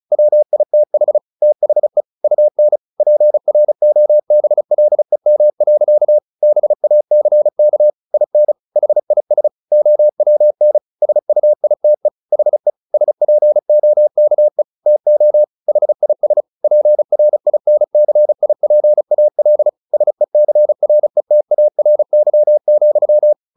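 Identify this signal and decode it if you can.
Morse code, heard as a keyed tone, about 34 wpm, 600 Hz. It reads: WITH THE UN PROBLEM. BACK IN HIS OWN SUITE HE SPOKE TO HIS PRINCIPAL SECRETARY,
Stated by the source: Morse practice at 35 wpm from CW med Gnister, sending Ken Follett's novel Never in English.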